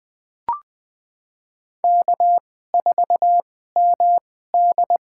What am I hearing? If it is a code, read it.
Morse code, heard as a keyed tone, about 20 wpm, spaced as for 14 wpm, 700 Hz. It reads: K4MD
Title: USA Call Signs - 4 Characters - 2x Character Spacing - 20wpm